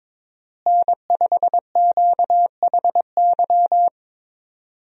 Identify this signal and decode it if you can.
Morse code, heard as a keyed tone, about 22 wpm, 700 Hz. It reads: N5QHY
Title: USA Call Signs - 5 Characters - Repeated 3x - 22wpm